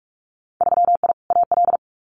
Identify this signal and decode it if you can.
Morse code, heard as a keyed tone, about 46 wpm, 700 Hz. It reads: 3I AL
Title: Winter Field Day Exchanges 45wpm